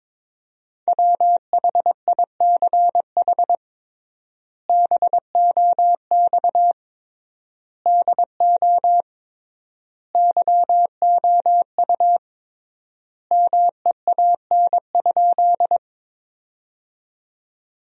Morse code, heard as a keyed tone, about 22 wpm, 700 Hz. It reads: WHICH BOX DO YOU MEAN?
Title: 3x Word-Spacing - Sentences from Top 500 Words - 22wpm